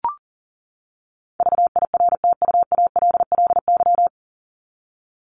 Morse code, heard as a keyed tone, about 40 wpm, 700 Hz. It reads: VIRTUALLY
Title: Top 1501-1750 Words 40wpm